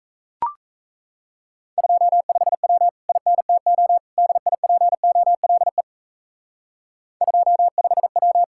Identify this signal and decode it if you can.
Morse code, heard as a keyed tone, about 35 wpm, 700 Hz. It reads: 25W INTO DIPOLE 25W